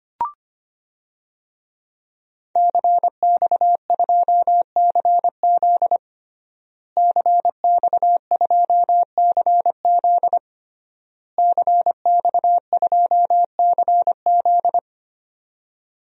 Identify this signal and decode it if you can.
Morse code, heard as a keyed tone, about 25 wpm, 700 Hz. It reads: CX2CZ CX2CZ CX2CZ
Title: DXCC-100 Call Signs - 5 Characters - Repeated 3x - 25wpm